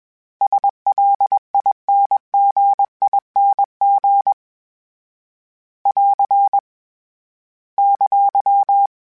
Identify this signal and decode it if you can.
Morse code, heard as a keyed tone, about 21 wpm, 800 Hz. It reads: SLINGING <AR> !